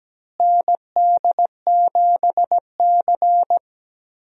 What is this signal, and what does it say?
Morse code, heard as a keyed tone, about 17 wpm, 700 Hz. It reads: ND7C
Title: USA Call Signs - 4 Characters - Repeated 3x - 17wpm